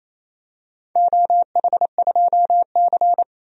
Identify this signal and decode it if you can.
Morse code, heard as a keyed tone, about 27 wpm, 700 Hz. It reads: OH2C
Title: DXCC-100 Call Signs - 4 Characters - 28wpm